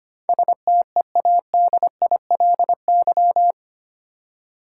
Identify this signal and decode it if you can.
Morse code, heard as a keyed tone, about 25 wpm, 690 Hz. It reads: STEADILY
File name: 8 Letter Words - First 3 Letters Given - 25wpm